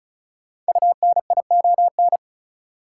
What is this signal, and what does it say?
Morse code, heard as a keyed tone, about 36 wpm, 700 Hz. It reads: UNION
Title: Sentences from Top 1400 Words - New Word Drill - 35wpm